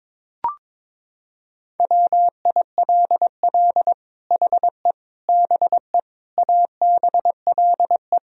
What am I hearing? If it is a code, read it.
Morse code, heard as a keyed tone, about 23 wpm, 700 Hz. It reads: WILL HE BE ABLE
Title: Sentences from Top 1000 Words - Review Part 3 - 22wpm